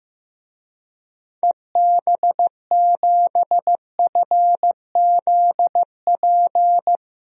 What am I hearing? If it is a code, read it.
Morse code, heard as a keyed tone, about 15 wpm, 700 Hz. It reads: EB7FZP
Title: DXCC-100 Call Signs - 6 Characters - 15wpm